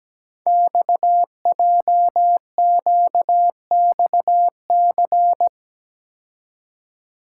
Morse code, heard as a keyed tone, about 17 wpm, 700 Hz. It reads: XJQXC
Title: Code Groups - 5 Letters - 17wpm